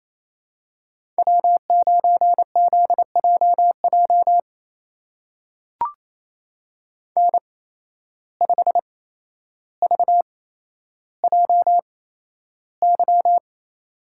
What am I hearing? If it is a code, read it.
Morse code, heard as a keyed tone, about 28 wpm, spaced as for 12 wpm, 700 Hz. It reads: W9ZJJ N5VJY
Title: USA Call Signs - 5 Characters - 8x Character Spacing - 28wpm